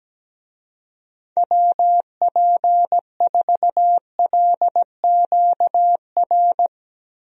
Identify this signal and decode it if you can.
Morse code, heard as a keyed tone, about 17 wpm, 700 Hz. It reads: WP4LQR
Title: USA Call Signs - 6 Characters - Repeated 3x - 17wpm